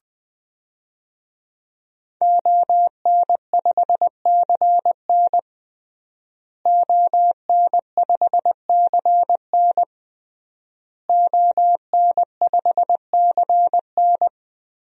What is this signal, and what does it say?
Morse code, heard as a keyed tone, about 20 wpm, 700 Hz. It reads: ON5CN ON5CN ON5CN
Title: DXCC-100 Call Signs - 5 Characters - Repeated 3x - 20wpm